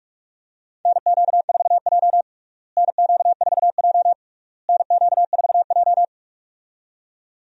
Morse code, heard as a keyed tone, about 46 wpm, 700 Hz. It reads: NQ4J NQ4J NQ4J
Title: USA Call Signs - 4 Characters - Repeated 3x - 45wpm